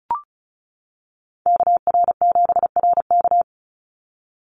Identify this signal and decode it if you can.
Morse code, heard as a keyed tone, about 34 wpm, 700 Hz. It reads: KR7RK
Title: NA SOTA Chaser Callsigns 35wpm